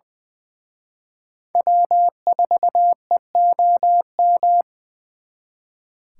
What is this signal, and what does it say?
Morse code, heard as a keyed tone, about 23 wpm, 700 Hz. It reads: W4EOM
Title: Speed Racing - USA 1x3 Call Signs 20wpm